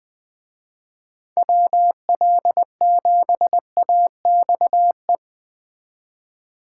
Morse code, heard as a keyed tone, about 20 wpm, 700 Hz. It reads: WL7AXE